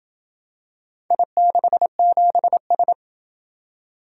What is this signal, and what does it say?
Morse code, heard as a keyed tone, about 27 wpm, 700 Hz. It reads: I67S